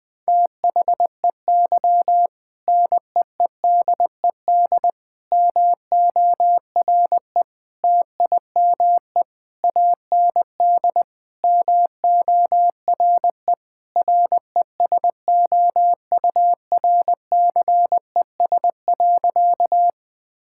Morse code, heard as a keyed tone, about 20 wpm, 700 Hz. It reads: THEY NEEDED MORE TIME AND MORE RESOURCES.
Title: Sentences with 2-Word Phrases - 20wpm